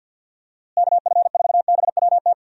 Morse code, heard as a keyed tone, about 51 wpm, 700 Hz. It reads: KF4BWT
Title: USA Call Signs - 6 Characters - Repeated 3x - 50wpm